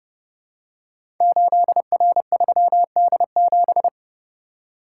Morse code, heard as a keyed tone, about 30 wpm, 700 Hz. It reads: 8R3D7
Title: Code Groups - 5 Letter:Numbers - 30wpm